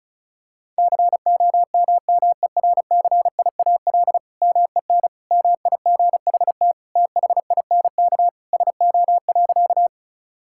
Morse code, heard as a keyed tone, about 36 wpm, 700 Hz. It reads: COMMERCIAL MEN MIGHT THINK SO.
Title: Sentences from Top 1000 Words - Review Part 5 - 35wpm